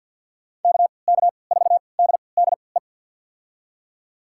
Morse code, heard as a keyed tone, about 48 wpm, spaced as for 28 wpm, 700 Hz. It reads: KK4DDE
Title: USA Call Signs - 6 Characters - 3x Character Spacing - 50wpm